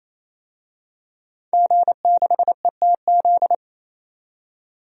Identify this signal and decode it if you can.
Morse code, heard as a keyed tone, about 28 wpm, 700 Hz. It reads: G6ETZ
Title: DXCC-100 Call Signs - 5 Characters - 28wpm